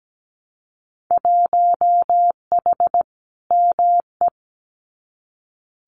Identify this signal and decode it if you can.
Morse code, heard as a keyed tone, about 17 wpm, 700 Hz. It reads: 1H ME
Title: Winter Field Day Exchanges 17wpm